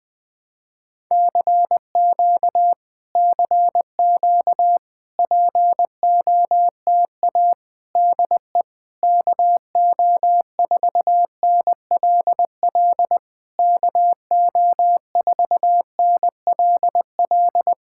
Morse code, heard as a keyed tone, about 20 wpm, 690 Hz. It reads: CQ CQ POTA DE KO4NLL KO4NLL
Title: POTA QSO Practice - 20wpm